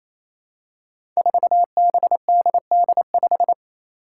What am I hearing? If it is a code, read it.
Morse code, heard as a keyed tone, about 28 wpm, 700 Hz. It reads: 4BDD5
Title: Five Challenging Letter-Number Combinations 28wpm